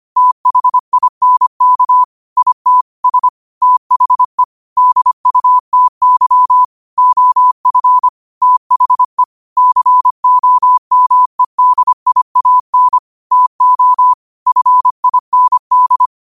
Morse code, heard as a keyed tone, about 25 wpm, 1000 Hz. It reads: THINK IT S THE DUTY OF THE COMEDIAN TO FIND
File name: qotdfiles / 25wpm